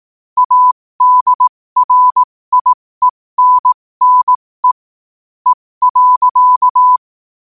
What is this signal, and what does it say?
Morse code, heard as a keyed tone, about 18 wpm, spaced as for 15 wpm, 1000 Hz. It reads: ADRIENNE E.
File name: qotdfiles / 15wpm